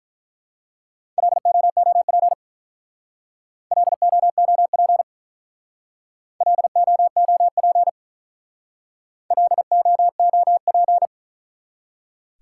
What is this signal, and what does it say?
Morse code, heard as a keyed tone, about 44 wpm, 700 Hz. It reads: LOOP LOOP LOOP LOOP